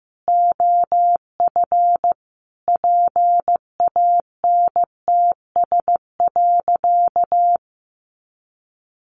Morse code, heard as a keyed tone, about 15 wpm, 700 Hz. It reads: OF PANTS.
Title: Additional Easy Phrases 15wpm